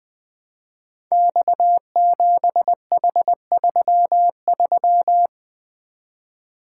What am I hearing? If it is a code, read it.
Morse code, heard as a keyed tone, about 20 wpm, 700 Hz. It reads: X7H33
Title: Code Groups - 5 Letter:Numbers - 20wpm